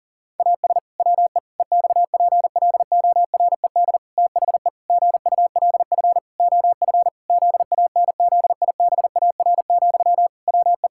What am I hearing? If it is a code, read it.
Morse code, heard as a keyed tone, about 37 wpm, 700 Hz. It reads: AS WE EXPLORED THE GULF OF ZANZIBAR, WE
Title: Pangram Challenge v2 - 40wpm